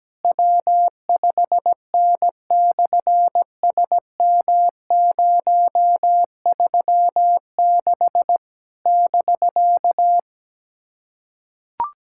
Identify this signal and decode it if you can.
Morse code, heard as a keyed tone, about 17 wpm, 700 Hz. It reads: W5N/SM036 <BK>
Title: SOTA QSO Practice 17wpm